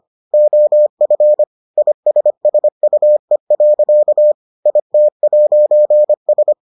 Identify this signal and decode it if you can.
Morse code, heard as a keyed tone, about 25 wpm, 600 Hz. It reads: OF ISSUE. IT'S